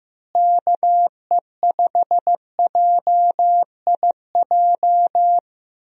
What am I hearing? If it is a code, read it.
Morse code, heard as a keyed tone, about 15 wpm, 700 Hz. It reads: KE5JIJ
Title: USA Call Signs - 6 Characters - Repeated 3x - 15wpm